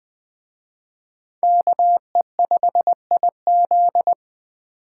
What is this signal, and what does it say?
Morse code, heard as a keyed tone, about 21 wpm, 700 Hz. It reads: KE5IZ